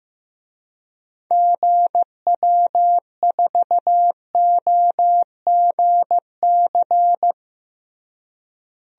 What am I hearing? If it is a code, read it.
Morse code, heard as a keyed tone, about 15 wpm, 700 Hz. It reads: GW4OGC
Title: DXCC-100 Call Signs - 6 Characters - Repeated 3x - 15wpm